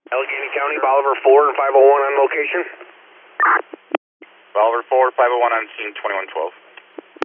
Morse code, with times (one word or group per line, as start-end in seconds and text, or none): none